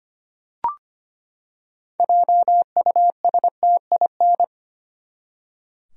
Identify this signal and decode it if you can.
Morse code, heard as a keyed tone, about 25 wpm, 700 Hz. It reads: JUSTIN
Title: USA Common Names - 25wpm